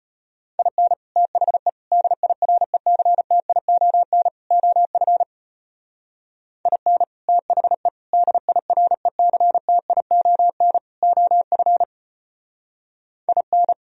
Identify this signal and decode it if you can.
Morse code, heard as a keyed tone, about 30 wpm, 700 Hz. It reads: IN THE DIRECTION OF IN THE DIRECTION OF IN
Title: Speed-Racing - Sets of 4 Words 25wpm